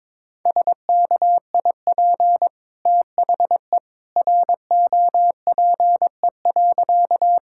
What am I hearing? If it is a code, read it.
Morse code, heard as a keyed tone, about 22 wpm, 690 Hz. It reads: SKIP THE ROPE.
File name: practice / 22wpm / Easy Sentences with Hints - 22wpm